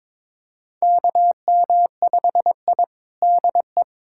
Morse code, heard as a keyed tone, about 22 wpm, 700 Hz. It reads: KM5I DE
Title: USA Callsign DE Callsign - 4 Characters - 22wpm